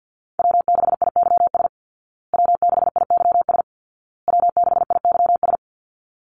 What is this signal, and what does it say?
Morse code, heard as a keyed tone, about 50 wpm, 700 Hz. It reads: W6IYS W6IYS W6IYS
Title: Call Signs Repeated 2x - 50wpm